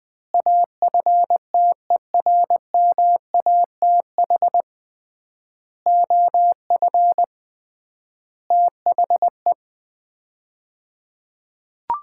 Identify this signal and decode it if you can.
Morse code, heard as a keyed tone, about 21 wpm, 700 Hz. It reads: AFTERMATH OF THE
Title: 3x Word-Spacing - Sets of 5 Words - 20wpm